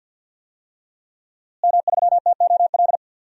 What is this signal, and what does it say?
Morse code, heard as a keyed tone, about 52 wpm, 700 Hz. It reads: M3TOL
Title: DXCC-100 Call Signs - 5 Characters - Repeated 3x - 50wpm